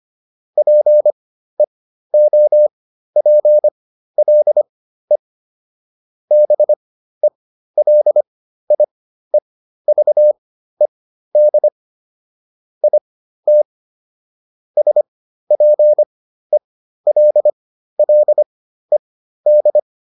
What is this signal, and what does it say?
Morse code, heard as a keyed tone, about 25 wpm, spaced as for 13 wpm, 600 Hz. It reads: PEOPLE BELIEVED IT SPELLED